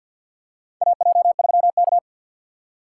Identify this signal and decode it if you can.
Morse code, heard as a keyed tone, about 51 wpm, 690 Hz. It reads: AJ3K
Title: USA Call Signs - 4 Characters - 6x Character Spacing - 50wpm